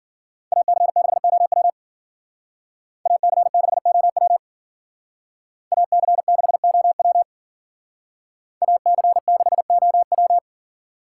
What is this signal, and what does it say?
Morse code, heard as a keyed tone, about 52 wpm, 700 Hz. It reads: AC6OW AC6OW AC6OW AC6OW